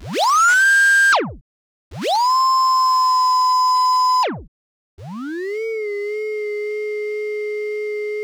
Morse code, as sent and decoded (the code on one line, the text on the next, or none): none